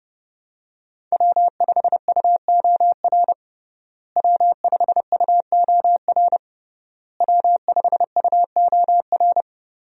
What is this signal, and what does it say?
Morse code, heard as a keyed tone, about 30 wpm, 700 Hz. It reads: W5UOR W5UOR W5UOR
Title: USA Call Signs - 5 Characters - Repeated 3x - 30wpm